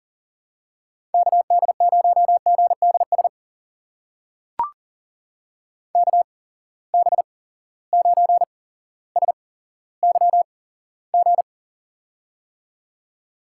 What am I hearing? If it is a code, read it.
Morse code, heard as a keyed tone, about 38 wpm, spaced as for 17 wpm, 700 Hz. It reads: KD0GDS KB9SYG
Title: USA Call Signs - 6 Characters - 8x Character Spacing - 40wpm